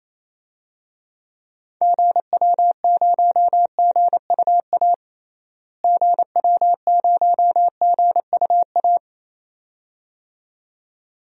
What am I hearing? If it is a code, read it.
Morse code, heard as a keyed tone, about 28 wpm, 700 Hz. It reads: GW0GUA GW0GUA
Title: DXCC-100 Call Signs - 6 Characters - Repeated 2x - 28wpm